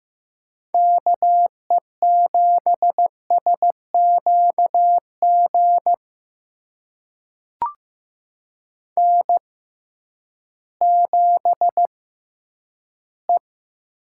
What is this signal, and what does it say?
Morse code, heard as a keyed tone, about 15 wpm, spaced as for 10 wpm, 700 Hz. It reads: KE7SQG N7E
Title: USA Call Signs - 6x Character Spacing - 15wpm